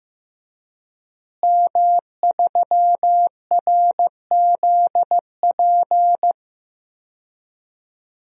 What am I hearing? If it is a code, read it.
Morse code, heard as a keyed tone, about 15 wpm, 700 Hz. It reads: M3RZP
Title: DXCC-100 Call Signs - 5 Characters - Repeated 2x - 15wpm